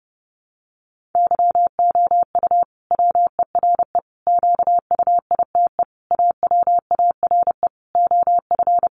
Morse code, heard as a keyed tone, about 30 wpm, 700 Hz. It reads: YOU WERE QUITE AWARE OF
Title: Sentences from Top 1200 Words 30wpm